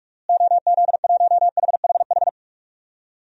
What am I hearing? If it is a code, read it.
Morse code, heard as a keyed tone, about 43 wpm, 700 Hz. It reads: OZ1HHH
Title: DXCC-100 Call Signs - 6 Characters - Repeated 2x - 45wpm